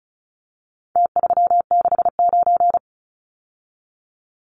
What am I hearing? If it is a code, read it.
Morse code, heard as a keyed tone, about 34 wpm, 700 Hz. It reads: T369
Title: Numbers - Contest Style - 4-Digit 35wpm